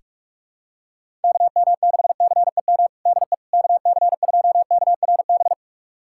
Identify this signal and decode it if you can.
Morse code, heard as a keyed tone, about 46 wpm, 700 Hz. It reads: KM6CEM DE KC2KRB
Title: USA Callsign DE Callsign - 6 Characters - 45wpm